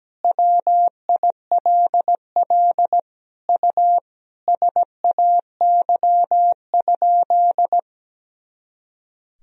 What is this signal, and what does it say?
Morse code, heard as a keyed tone, about 17 wpm, 700 Hz. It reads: WILL U SAY?
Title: Sentences from Top 200 Words and Abbreviations 17wpm